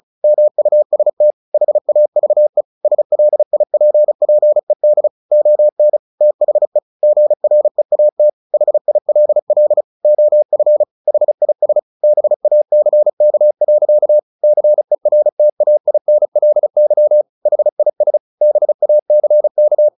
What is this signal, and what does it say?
Morse code, heard as a keyed tone, about 34 wpm, 600 Hz. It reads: MUST HAVE SLIPPED ON THE GREAT HILL OF HIS BACK. CERTAINLY HIS BACK
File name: the old man and the sea / the old man and the sea 1 35wpm